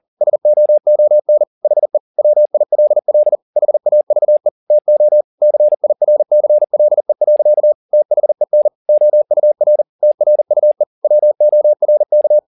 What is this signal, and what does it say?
Morse code, heard as a keyed tone, about 40 wpm, 600 Hz. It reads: SOON HE WILL HAVE TO CIRCLE. THEN OUR TRUE WORK
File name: the old man and the sea 2 40wpm